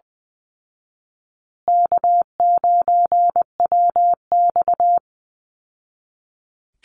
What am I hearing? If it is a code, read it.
Morse code, heard as a keyed tone, about 20 wpm, 700 Hz. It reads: K9WX
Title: Call Signs Repeated 3x - 20wpm